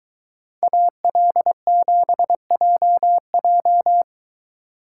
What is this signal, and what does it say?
Morse code, heard as a keyed tone, about 23 wpm, 700 Hz. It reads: AL7JJ